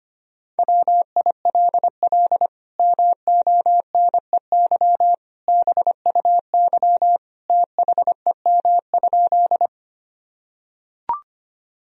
Morse code, heard as a keyed tone, about 25 wpm, 700 Hz. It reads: WILL MONEY BUY THEM?
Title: Sentences from Top 1000 Words - Review Part 4 - 25wpm